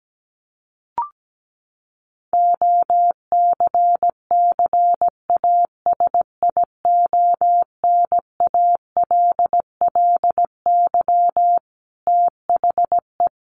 Morse code, heard as a keyed tone, about 17 wpm, 700 Hz. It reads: OCCASIONALLY THE
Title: Sentences from Top 1300 Words 17wpm